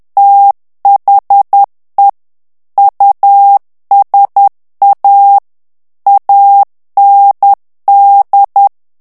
Morse code, heard as a keyed tone, about 10 wpm, 800 Hz. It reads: THE USA AND
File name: morse mp3 / ANGLAIS texte clair 2 vitesse 10 mots minute